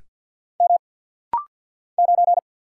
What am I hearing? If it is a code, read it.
Morse code, heard as a keyed tone, about 50 wpm, 700 Hz. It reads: M 9
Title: Single Letter-Number - Rapid-Fire 50wpm